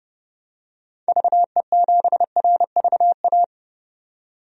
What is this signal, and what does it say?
Morse code, heard as a keyed tone, about 32 wpm, 700 Hz. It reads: VE7RVA
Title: DXCC-100 Call Signs - 6 Characters - 30wpm